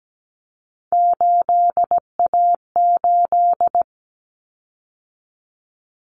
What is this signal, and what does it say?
Morse code, heard as a keyed tone, about 17 wpm, 700 Hz. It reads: 8A8
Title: Common Cut Numbers - 3-Digit 17wpm